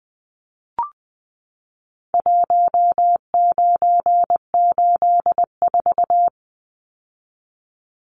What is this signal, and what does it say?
Morse code, heard as a keyed tone, about 20 wpm, 700 Hz. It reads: 1984
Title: Numbers - Contest Style - 4-Digit 20wpm